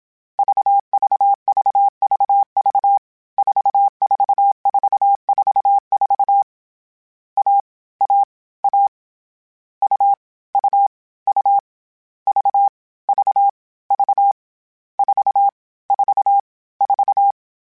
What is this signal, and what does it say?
Morse code, heard as a keyed tone, about 27 wpm, 800 Hz. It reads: VVVVV 44444 A A A U U U V V V 4 4 4
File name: Short Runs / Short Runs-mp3s / AUV AUV4 @25